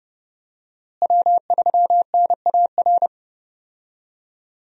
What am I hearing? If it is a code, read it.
Morse code, heard as a keyed tone, about 32 wpm, 700 Hz. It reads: W3NAR
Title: USA Call Signs - 5 Characters - Repeated 3x - 30wpm